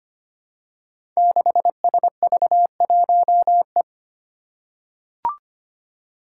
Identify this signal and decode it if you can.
Morse code, heard as a keyed tone, about 25 wpm, 700 Hz. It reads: 6SV1E
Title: Five Challenging Letter-Number Combinations 25wpm